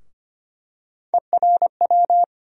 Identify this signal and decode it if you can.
Morse code, heard as a keyed tone, about 25 wpm, 700 Hz. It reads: ERW
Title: USA Call Sign Postfixes - 3 Letters 25wpm